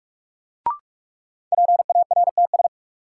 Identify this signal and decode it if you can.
Morse code, heard as a keyed tone, about 46 wpm, 700 Hz. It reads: PARTS
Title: Sets of 3 Words - New Word Drill - 45wpm